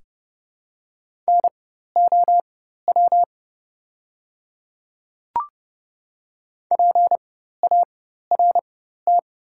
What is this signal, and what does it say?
Morse code, heard as a keyed tone, about 30 wpm, spaced as for 14 wpm, 700 Hz. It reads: NOW PART
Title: 4x Character Spacing - Top 300 Words - 30wpm